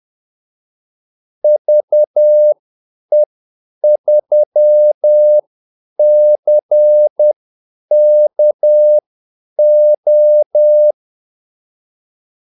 Morse code, heard as a keyed tone, about 10 wpm, spaced as for 8 wpm, 600 Hz. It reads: VE3CKO
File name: Callsigns-10-8wpm